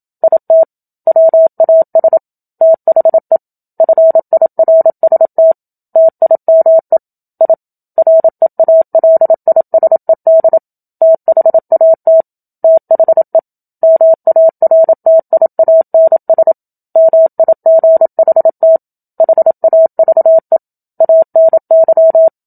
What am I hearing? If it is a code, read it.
Morse code, heard as a keyed tone, about 27 wpm, 650 Hz. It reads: IT WAS THE FIRST TIME I REALISED THAT THE MARTIANS MIGHT HAVE ANY
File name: Book Two / 01-Chapter 1 - 27 WPM